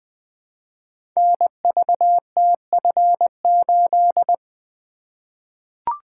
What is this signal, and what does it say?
Morse code, heard as a keyed tone, about 20 wpm, 700 Hz. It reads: NVTF8
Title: Code Groups - 5 Letter:Numbers - 20wpm